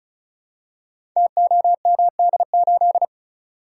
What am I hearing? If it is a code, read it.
Morse code, heard as a keyed tone, about 36 wpm, 690 Hz. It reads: TOMD8
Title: Code Groups - 5 Letter:Numbers - 35wpm